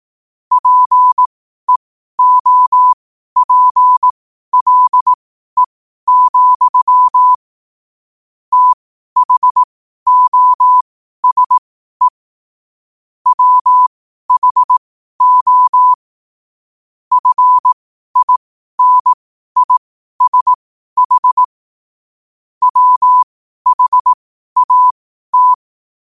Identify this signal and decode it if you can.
Morse code, heard as a keyed tone, about 18 wpm, spaced as for 12 wpm, 1000 Hz. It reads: PEOPLE, THOSE WHO FINISH WHAT
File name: 12wpm